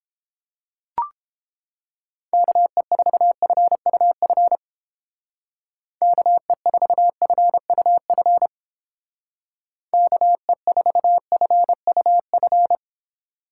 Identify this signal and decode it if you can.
Morse code, heard as a keyed tone, about 29 wpm, 700 Hz. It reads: KE4FUF KE4FUF KE4FUF